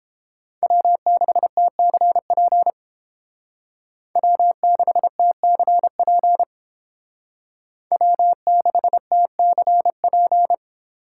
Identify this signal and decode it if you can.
Morse code, heard as a keyed tone, about 30 wpm, 700 Hz. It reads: W6TCP W6TCP W6TCP